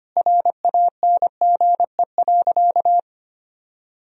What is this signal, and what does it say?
Morse code, heard as a keyed tone, about 25 wpm, 700 Hz. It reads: RANGE.
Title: Sentences from Top 1000 Words - Review Part 4 - 25wpm